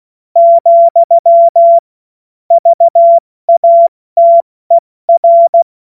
Morse code, heard as a keyed tone, about 16 wpm, 680 Hz. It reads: , VATER
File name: Witz-08-Juli-2025-70-F80